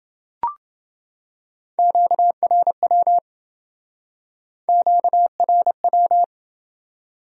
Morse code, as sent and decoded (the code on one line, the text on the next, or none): --.- .-. .-- / --.- .-. .--
QRW QRW